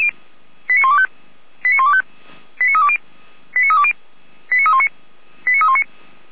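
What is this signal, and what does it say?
Five-tone selective call (ZVEI), heard as quick runs of five tones, tones of about 70 ms. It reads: 89125 89125 89320 89320 89319 89319
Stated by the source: ZVEI five-tone selective call received by SDR.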